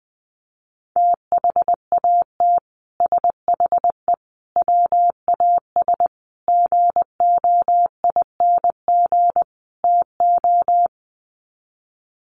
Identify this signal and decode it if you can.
Morse code, heard as a keyed tone, about 20 wpm, 700 Hz. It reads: THAT SHE WAS GOING TO
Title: Sets of 5 Words Encore2 20wpm